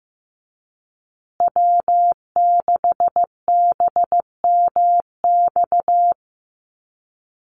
Morse code, heard as a keyed tone, about 15 wpm, 700 Hz. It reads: W6BMX
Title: USA Call Signs Repeated 3x 15wpm